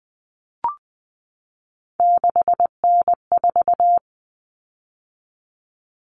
Morse code, heard as a keyed tone, about 20 wpm, 700 Hz. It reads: 6N4
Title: Common Cut Numbers - 3-Digit 20wpm